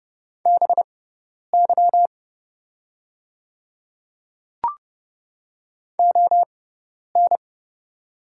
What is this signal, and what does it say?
Morse code, heard as a keyed tone, about 30 wpm, spaced as for 10 wpm, 700 Hz. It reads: BY ON
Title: 6x Character Spacing - Top 300 Words - 30wpm